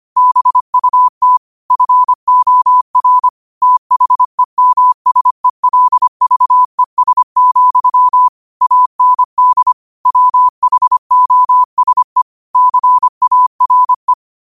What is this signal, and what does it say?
Morse code, heard as a keyed tone, about 25 wpm, 1000 Hz. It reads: BUT FOR THEMSELVES, AND WHOSE CARE